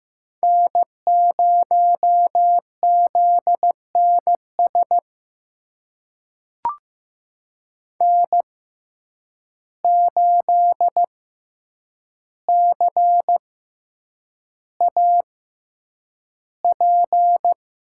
Morse code, heard as a keyed tone, about 15 wpm, spaced as for 8 wpm, 700 Hz. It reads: N0ZNS N8CAP
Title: USA Call Signs - 5 Characters - 6x Character Spacing - 15wpm